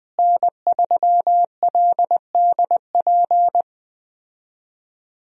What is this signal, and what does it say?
Morse code, heard as a keyed tone, about 20 wpm, 700 Hz. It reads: N3LDP
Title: USA Call Signs - 5 Characters - Repeated 3x - 20wpm